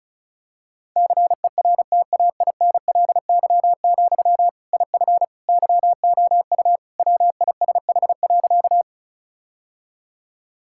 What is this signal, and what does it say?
Morse code, heard as a keyed tone, about 36 wpm, 700 Hz. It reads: CERTAINLY, IF YOU WISH.
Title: Sentences from Top 1600 Words 35wpm